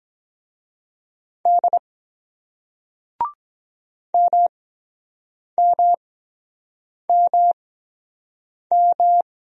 Morse code, heard as a keyed tone, about 23 wpm, spaced as for 7 wpm, 700 Hz. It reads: D MMMM